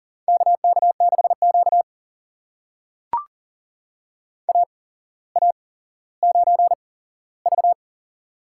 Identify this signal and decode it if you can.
Morse code, heard as a keyed tone, about 38 wpm, spaced as for 17 wpm, 700 Hz. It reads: KK6Q AA9V
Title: USA Call Signs - 4 Characters - 8x Character Spacing - 40wpm